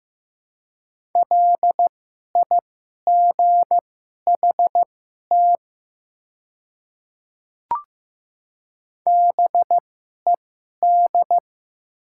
Morse code, heard as a keyed tone, about 15 wpm, spaced as for 11 wpm, 690 Hz. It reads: LIGHT BED
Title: 2x Character Spacing - Top 500 Words - 15wpm